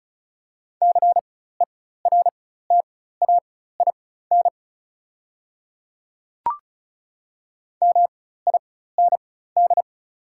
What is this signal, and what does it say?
Morse code, heard as a keyed tone, about 34 wpm, spaced as for 16 wpm, 700 Hz. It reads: CERTAIN MIND
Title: 4x Character Spacing - Top 500 Words - 35wpm